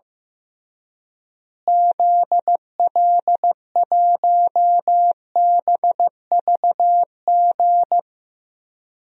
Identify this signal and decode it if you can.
Morse code, heard as a keyed tone, about 15 wpm, 700 Hz. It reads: ZL1BVG
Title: DXCC-100 Call Signs - 6 Characters - Repeated 3x - 15wpm